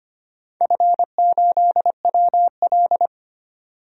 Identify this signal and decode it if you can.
Morse code, heard as a keyed tone, about 25 wpm, 700 Hz. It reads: F8WL
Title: DXCC-100 Call Signs - 4 Characters - 25wpm